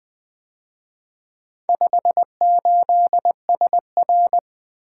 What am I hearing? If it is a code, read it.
Morse code, heard as a keyed tone, about 24 wpm, 700 Hz. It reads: 58SR
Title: Speed-Racing - Four Letter-Number Combinations - 20wpm